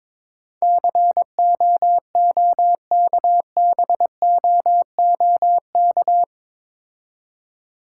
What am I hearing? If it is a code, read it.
Morse code, heard as a keyed tone, about 22 wpm, 700 Hz. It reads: COOKBOOK
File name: Difficult Words with Hints - 22wpm